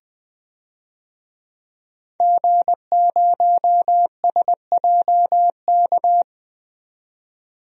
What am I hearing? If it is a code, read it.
Morse code, heard as a keyed tone, about 20 wpm, 700 Hz. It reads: G0SJK
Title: DXCC-100 Call Signs - 5 Characters - 20wpm